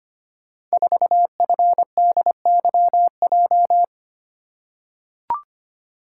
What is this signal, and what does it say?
Morse code, heard as a keyed tone, about 25 wpm, 700 Hz. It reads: 4FDYJ
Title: Code Groups - 5 Letter:Numbers - 25wpm